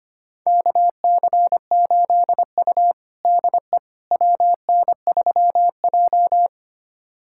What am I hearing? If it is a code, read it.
Morse code, heard as a keyed tone, about 25 wpm, 700 Hz. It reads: KC8U DE WN3J
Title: USA Callsign DE Callsign - 4 Characters - 25wpm